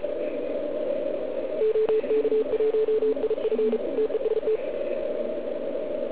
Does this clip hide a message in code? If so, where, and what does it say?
Yes: Morse code from 1.6 to 4.6 s, heard as a keyed tone, about 34 wpm, 440 Hz. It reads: OK1IF TEST